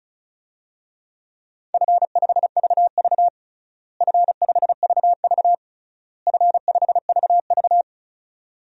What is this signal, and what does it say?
Morse code, heard as a keyed tone, about 36 wpm, 700 Hz. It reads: F5VV F5VV F5VV
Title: DXCC-100 Call Signs - 4 Characters - Repeated 3x - 35wpm